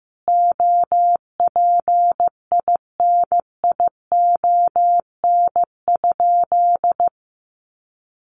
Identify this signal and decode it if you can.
Morse code, heard as a keyed tone, about 15 wpm, 700 Hz. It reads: OPINION?
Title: Sentences from Top 1200 Words 15wpm